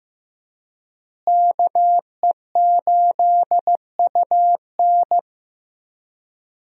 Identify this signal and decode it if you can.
Morse code, heard as a keyed tone, about 15 wpm, 700 Hz. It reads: KE8UN
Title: USA Call Signs - 5 Characters - Repeated 2x - 15wpm